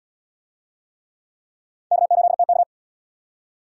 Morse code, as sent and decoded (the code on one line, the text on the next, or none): -.- ---.. - -.-
K8TK